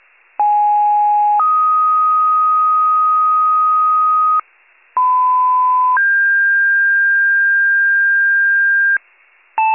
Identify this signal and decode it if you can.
Two-tone page, heard as a one-second tone, then a three-second tone, about 830 then 1250 Hz, about 1010 then 1600 Hz, with noise. It is heard from 0.4 to 9.0 s.